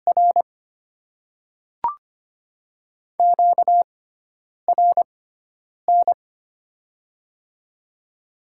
Morse code, heard as a keyed tone, about 25 wpm, spaced as for 9 wpm, 700 Hz. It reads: R QRN
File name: Top 90 QSO Elements - 6x Character Spacing - 25wpm